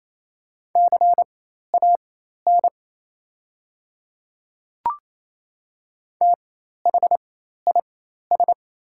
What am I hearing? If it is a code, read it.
Morse code, heard as a keyed tone, about 28 wpm, spaced as for 13 wpm, 700 Hz. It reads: CAN THIS
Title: 4x Character Spacing - Top 100 Words - 28wpm